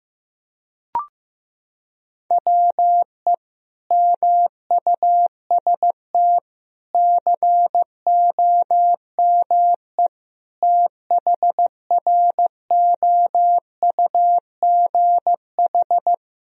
Morse code, heard as a keyed tone, about 15 wpm, 700 Hz. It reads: WE MUST COME THROUGH
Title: Sentences from Top 1000 Words - Review Part 1 - 15wpm